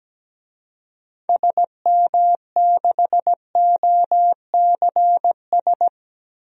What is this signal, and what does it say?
Morse code, heard as a keyed tone, about 17 wpm, 700 Hz. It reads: SM6OCS
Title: DXCC-100 Call Signs - 6 Characters - 17wpm